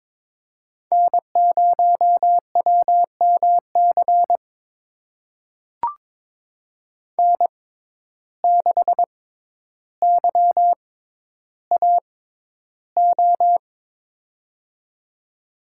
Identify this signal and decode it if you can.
Morse code, heard as a keyed tone, about 23 wpm, spaced as for 11 wpm, 700 Hz. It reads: N0WMC N6YAO
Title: USA Call Signs - 5 Characters - 6x Character Spacing - 22wpm